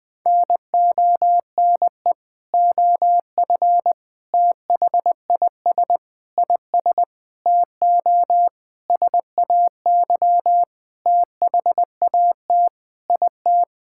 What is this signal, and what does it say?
Morse code, heard as a keyed tone, about 20 wpm, 700 Hz. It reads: NONE OF THIS IS TO SAY THAT IT
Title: Sentences with 7-Word Phrases - 20wpm